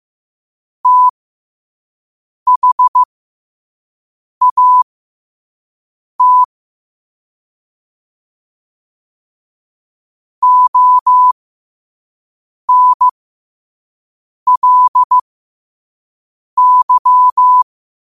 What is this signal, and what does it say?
Morse code, heard as a keyed tone, about 15 wpm, spaced as for 5 wpm, 1000 Hz. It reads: THAT ONLY
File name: 2026-03-16-5wpm